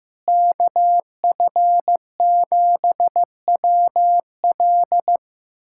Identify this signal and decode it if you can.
Morse code, heard as a keyed tone, about 15 wpm, 700 Hz. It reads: KF7WL
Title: USA Callsign DE Callsign - 5 Characters - 15wpm